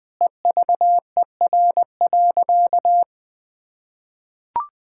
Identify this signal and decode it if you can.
Morse code, heard as a keyed tone, about 21 wpm, 700 Hz. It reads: EVER.
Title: Sentences with 5-Word Phrases - 20wpm